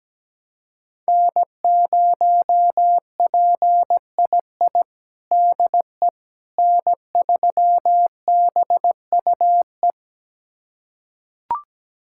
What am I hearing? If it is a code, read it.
Morse code, heard as a keyed tone, about 17 wpm, 700 Hz. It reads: N0PII DE N3BUE
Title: USA Callsign DE Callsign - 5 Characters - 17wpm